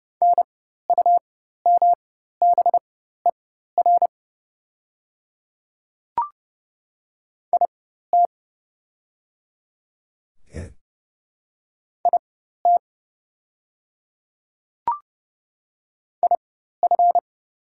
Morse code, heard as a keyed tone, about 30 wpm, spaced as for 14 wpm, 700 Hz. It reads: NUMBER IT IT IF
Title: 4x Character Spacing - Top 100 Words - 30wpm